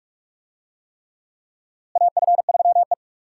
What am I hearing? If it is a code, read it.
Morse code, heard as a keyed tone, about 46 wpm, 700 Hz. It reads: AF3E
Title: USA Call Signs - 4 Characters - Repeated 3x - 45wpm